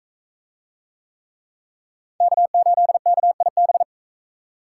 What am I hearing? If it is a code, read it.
Morse code, heard as a keyed tone, about 43 wpm, 700 Hz. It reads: K8KIB